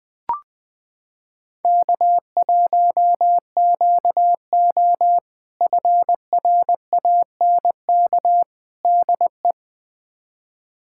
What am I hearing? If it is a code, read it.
Morse code, heard as a keyed tone, about 21 wpm, 700 Hz. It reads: K1QO FRANK DE
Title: K1USN SST QSO Practice - 20wpm